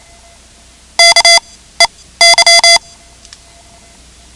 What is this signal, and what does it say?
Morse code, heard as a keyed tone, about 28 wpm, spaced as for 16 wpm, 2050 Hz. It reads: KEY
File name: ham radio / CodeQuick / Code Quick 10